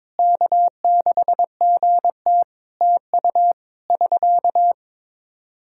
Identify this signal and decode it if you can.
Morse code, heard as a keyed tone, about 22 wpm, 700 Hz. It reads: K6GT TU <SK>
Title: Practice QSO - Exchange 4 - Ending - 22wpm